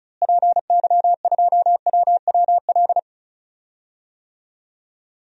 Morse code, heard as a keyed tone, about 36 wpm, 700 Hz. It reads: PY2WWL
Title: DXCC-100 Call Signs - 6 Characters - Repeated 2x - 35wpm